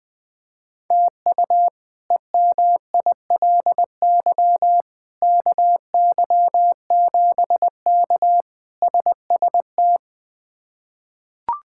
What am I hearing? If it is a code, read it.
Morse code, heard as a keyed tone, about 21 wpm, 700 Hz. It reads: TU EMILY KY7K SST
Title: K1USN SST QSO Practice - 20wpm